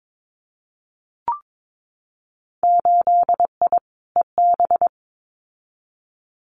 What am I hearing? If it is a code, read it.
Morse code, heard as a keyed tone, about 22 wpm, 700 Hz. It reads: 8I EB
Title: Winter Field Day Exchanges Encore 22wpm